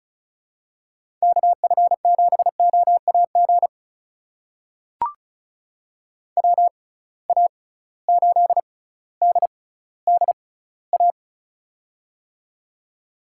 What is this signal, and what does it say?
Morse code, heard as a keyed tone, about 36 wpm, spaced as for 18 wpm, 700 Hz. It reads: KF7OAG WA8DDA
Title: USA Call Signs - 6 Characters - 6x Character Spacing - 35wpm